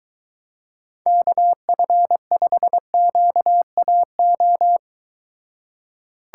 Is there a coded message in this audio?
Yes: Morse code from 1.1 to 4.8 s, heard as a keyed tone, about 23 wpm, 700 Hz. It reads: KF5QAO